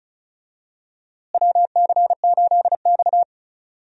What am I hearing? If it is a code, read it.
Morse code, heard as a keyed tone, about 34 wpm, 700 Hz. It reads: WC8X
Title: USA Call Signs - 4 Characters - Repeated 3x - 35wpm